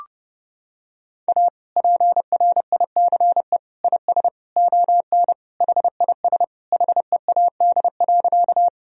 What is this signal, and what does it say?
Morse code, heard as a keyed tone, about 32 wpm, 700 Hz. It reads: A PRICE IS ON HIS HEAD.
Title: Sentences from Top 1000 Words - Review Part 4 - 30wpm